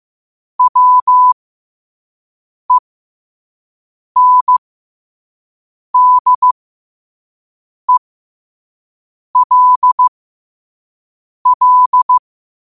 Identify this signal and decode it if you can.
Morse code, heard as a keyed tone, about 15 wpm, spaced as for 5 wpm, 1000 Hz. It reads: WENDELL
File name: qotdfiles / 5wpm